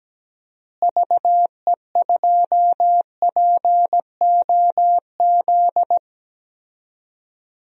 Morse code, heard as a keyed tone, about 17 wpm, 700 Hz. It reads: VE2POZ
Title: DXCC-100 Call Signs - 6 Characters - Repeated 2x - 17wpm